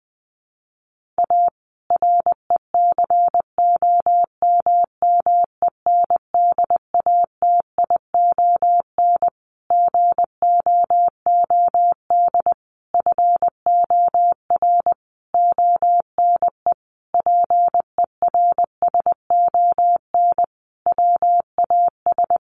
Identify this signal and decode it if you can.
Morse code, heard as a keyed tone, about 20 wpm, 700 Hz. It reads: A RECOMMENDATION GOOD FOR ONE PERSON WAS
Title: Sentences from Top 1400 Words 20wpm